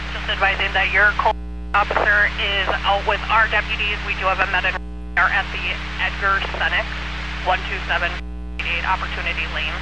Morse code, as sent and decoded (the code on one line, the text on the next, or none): none